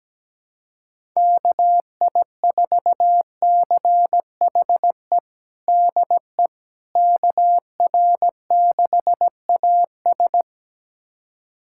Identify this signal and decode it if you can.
Morse code, heard as a keyed tone, about 17 wpm, 700 Hz. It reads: KI4CHE DE KR6AS
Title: USA Callsign DE Callsign 17wpm